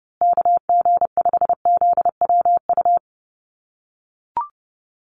Morse code, heard as a keyed tone, about 30 wpm, 700 Hz. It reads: KG5ZWU
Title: USA Call Signs Repeated 3x 30wpm